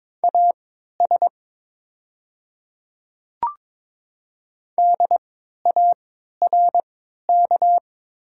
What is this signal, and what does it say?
Morse code, heard as a keyed tone, about 23 wpm, spaced as for 13 wpm, 690 Hz. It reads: AS DARK
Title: 3x Character Spacing - Top 500 Words - 22wpm